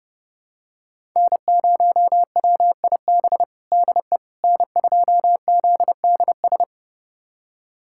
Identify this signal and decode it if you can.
Morse code, heard as a keyed tone, about 30 wpm, 700 Hz. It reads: N0WIB DE N2ZDS
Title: USA Callsign DE Callsign - 5 Characters - 30wpm